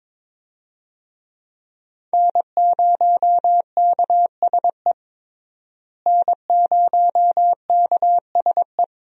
Morse code, heard as a keyed tone, about 22 wpm, 700 Hz. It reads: N0KSE N0KSE
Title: USA Call Signs - 5 Characters - Repeated 2x - 22wpm